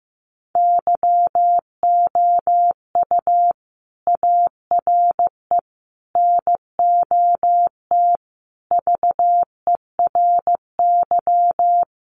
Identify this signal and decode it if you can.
Morse code, heard as a keyed tone, about 15 wpm, 700 Hz. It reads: YOU ARE NOT VERY
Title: Sentences from Top 1200 Words 15wpm